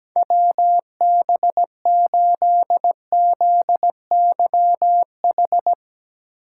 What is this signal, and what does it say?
Morse code, heard as a keyed tone, about 17 wpm, 700 Hz. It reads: WB8ZYH
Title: USA Call Signs - 6 Characters - 8x Character Spacing - 17wpm